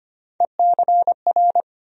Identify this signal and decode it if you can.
Morse code, heard as a keyed tone, about 25 wpm, 700 Hz. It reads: ECR
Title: USA Call Sign Postfixes - 3 Letters 25wpm